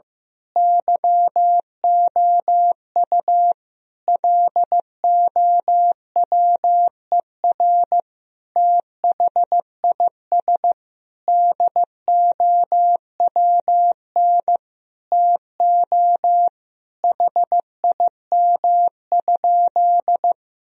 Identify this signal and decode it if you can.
Morse code, heard as a keyed tone, about 15 wpm, 700 Hz. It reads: YOU LOWER THIS DOWN TO HIM?
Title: Sentences from Top 1000 Words - Review Part 4 - 15wpm